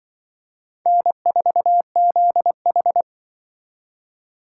Morse code, heard as a keyed tone, about 25 wpm, 700 Hz. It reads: N4ZH
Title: Speed Racing - USA Call Signs 20wpm